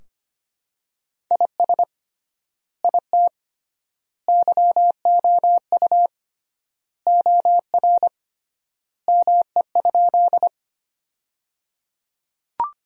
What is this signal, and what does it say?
Morse code, heard as a keyed tone, about 25 wpm, 700 Hz. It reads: IS IT YOU OR ME?
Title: 3x Word-Spacing - Sentences from Top 200 Words - 25wpm